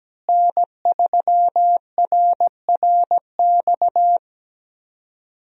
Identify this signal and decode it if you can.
Morse code, heard as a keyed tone, about 17 wpm, 700 Hz. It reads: N3RRX
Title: USA Call Signs - 5 Characters - Repeated 2x - 17wpm